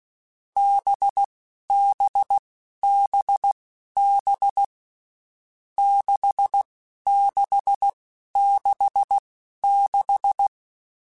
Morse code, heard as a keyed tone, about 16 wpm, spaced as for 11 wpm, 800 Hz. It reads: BBBB 6666
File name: Groups@10 / Groups@10 mp3s / VB46@10